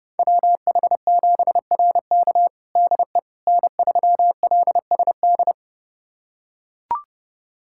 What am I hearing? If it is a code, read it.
Morse code, heard as a keyed tone, about 30 wpm, 700 Hz. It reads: WH7RK DE N3LSD
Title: USA Callsign DE Callsign - 5 Characters - 30wpm